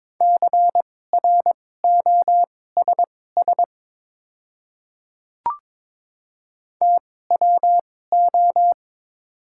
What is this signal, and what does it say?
Morse code, heard as a keyed tone, about 22 wpm, spaced as for 16 wpm, 700 Hz. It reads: CROSS TWO
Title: 2x Character Spacing - Top 300 Words - 22wpm